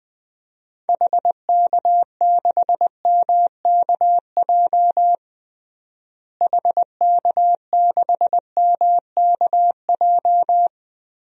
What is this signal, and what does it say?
Morse code, heard as a keyed tone, about 20 wpm, 700 Hz. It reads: HK6MKJ HK6MKJ
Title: DXCC-100 Call Signs - 6 Characters - Repeated 3x - 20wpm